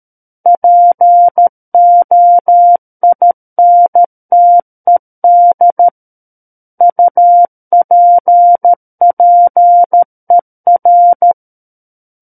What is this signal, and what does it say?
Morse code, heard as a keyed tone, about 13 wpm, 700 Hz. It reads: POINTED UPPER
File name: Book One / 04-Chapter 4 - 13 WPM